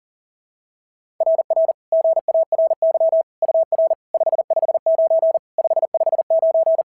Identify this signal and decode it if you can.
Morse code, heard as a keyed tone, about 40 wpm, 690 Hz. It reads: RR GARY UR 559 559
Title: SOTA QSO Practice 40wpm